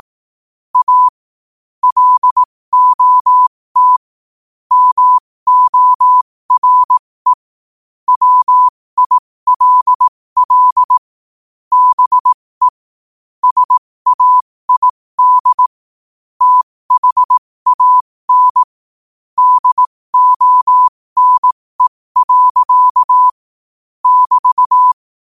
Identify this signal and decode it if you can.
Morse code, heard as a keyed tone, about 18 wpm, spaced as for 15 wpm, 1000 Hz. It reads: A LOT MORE WILL BE SAID THAN DONE. <BT>
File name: qotdfiles / 15wpm